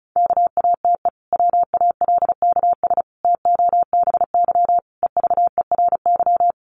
Morse code, heard as a keyed tone, about 34 wpm, 700 Hz. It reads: KATE WALKS TOBY EVERY
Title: Easy Phrases 35wpm